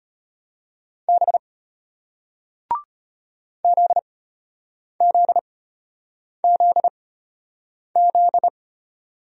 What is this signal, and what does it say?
Morse code, heard as a keyed tone, about 33 wpm, spaced as for 8 wpm, 700 Hz. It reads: B ZZZZ